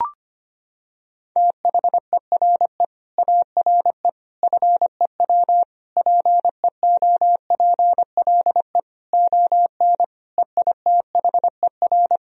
Morse code, heard as a keyed tone, about 25 wpm, 700 Hz. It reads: THERE ARE FEW PEOPLE ON EITHER
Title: Sentences from Top 1000 Words - Review Part 4 - 25wpm